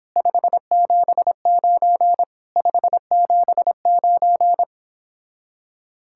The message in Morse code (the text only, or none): 579 579